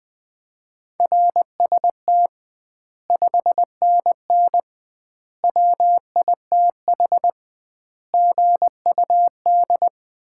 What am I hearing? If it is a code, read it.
Morse code, heard as a keyed tone, about 20 wpm, 700 Hz. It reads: RST 5NN WITH GUD
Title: Common QSO Phrases - 2x Word-Spacing - 20wpm